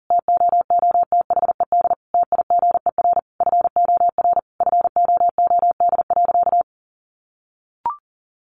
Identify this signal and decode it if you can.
Morse code, heard as a keyed tone, about 40 wpm, 700 Hz. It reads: TOOTHED TIGER FOR FOOD.
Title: A Sound Of Thunder 40wpm